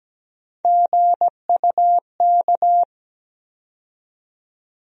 Morse code, heard as a keyed tone, about 17 wpm, 700 Hz. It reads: GUK
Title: USA Call Sign Postfixes - 3 Letters 17wpm